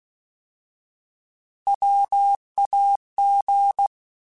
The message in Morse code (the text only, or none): WAG